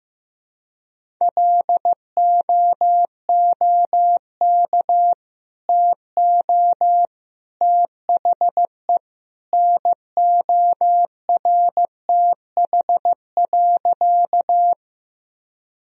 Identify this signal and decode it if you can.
Morse code, heard as a keyed tone, about 15 wpm, 700 Hz. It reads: LOOK TO THE NORTH.
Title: Sentences from Top 1000 Words - Review Part 1 - 15wpm